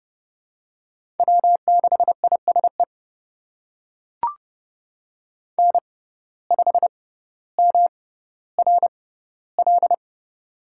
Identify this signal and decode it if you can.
Morse code, heard as a keyed tone, about 30 wpm, spaced as for 15 wpm, 700 Hz. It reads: W6ISE N5MRL
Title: USA Call Signs - 6x Character Spacing - 30wpm